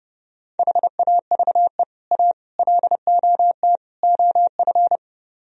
Morse code, heard as a keyed tone, about 29 wpm, 700 Hz. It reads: HAVE A LOT OF